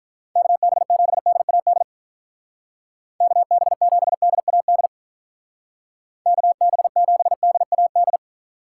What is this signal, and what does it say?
Morse code, heard as a keyed tone, about 44 wpm, 700 Hz. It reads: KB7DAD KB7DAD KB7DAD